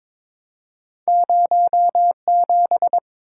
Morse code, heard as a keyed tone, about 22 wpm, 700 Hz. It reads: 07
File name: practice / 22wpm / Two Letter-Number Combinations - 22wpm